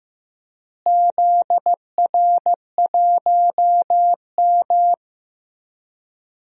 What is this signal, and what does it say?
Morse code, heard as a keyed tone, about 15 wpm, 700 Hz. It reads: ZR1M
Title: DXCC-100 Call Signs - 4 Characters - Repeated 3x - 15wpm